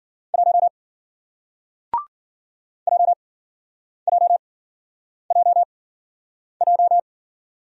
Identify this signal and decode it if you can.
Morse code, heard as a keyed tone, about 54 wpm, spaced as for 9 wpm, 700 Hz. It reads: 1 JJJJ